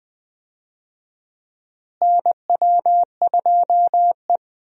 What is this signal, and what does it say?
Morse code, heard as a keyed tone, about 20 wpm, 700 Hz. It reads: NW2E
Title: USA Call Signs - 4 Characters - Repeated 3x - 20wpm